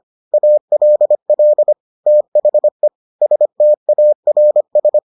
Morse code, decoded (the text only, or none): ALL THE STARS